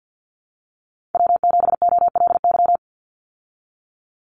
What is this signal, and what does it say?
Morse code, heard as a keyed tone, about 50 wpm, 700 Hz. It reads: W7OLY
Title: Call Signs Repeated 3x - 50wpm